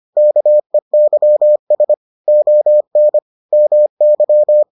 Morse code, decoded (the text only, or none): KEYS ON MY